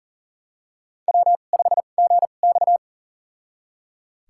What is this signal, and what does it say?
Morse code, heard as a keyed tone, about 39 wpm, spaced as for 29 wpm, 700 Hz. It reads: W5GX
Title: USA Call Signs - 4 Characters - 2x Character Spacing - 40wpm